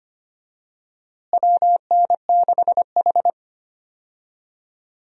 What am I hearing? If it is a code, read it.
Morse code, heard as a keyed tone, about 25 wpm, 700 Hz. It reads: WN6H
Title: USA Call Signs - 4 Characters - Repeated 2x - 25wpm